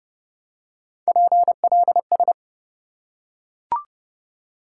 Morse code, heard as a keyed tone, about 30 wpm, 700 Hz. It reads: PLS
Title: Top 90 QSO Elements 30wpm